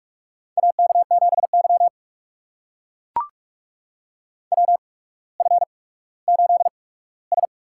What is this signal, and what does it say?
Morse code, heard as a keyed tone, about 44 wpm, spaced as for 19 wpm, 700 Hz. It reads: AK7Y WF8S
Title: USA Call Signs - 4 Characters - 8x Character Spacing - 45wpm